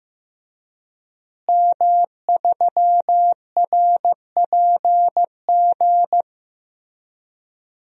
Morse code, heard as a keyed tone, about 15 wpm, 700 Hz. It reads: M3RPG
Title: DXCC-100 Call Signs - 5 Characters - Repeated 2x - 15wpm